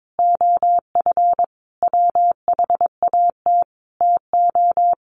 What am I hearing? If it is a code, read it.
Morse code, heard as a keyed tone, about 22 wpm, 700 Hz. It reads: OF WHAT TO
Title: Sentences from Top 1400 Words 22wpm